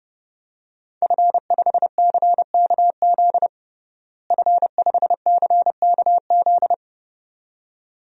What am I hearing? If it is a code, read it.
Morse code, heard as a keyed tone, about 31 wpm, 700 Hz. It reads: F5CKZ F5CKZ
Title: DXCC-100 Call Signs - 5 Characters - Repeated 2x - 30wpm